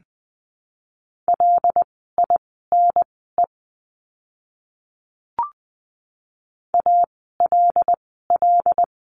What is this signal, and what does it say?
Morse code, heard as a keyed tone, about 20 wpm, spaced as for 14 wpm, 700 Hz. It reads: LINE ALL
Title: 115 Lesson 19f - Letter w - Words 20wpm